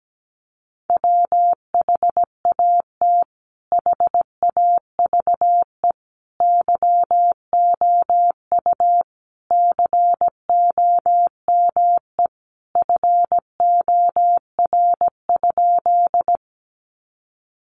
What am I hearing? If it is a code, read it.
Morse code, heard as a keyed tone, about 17 wpm, 700 Hz. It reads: WHAT HAVE YOU COME FOR?
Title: Sentences from Top 100 Words 17wpm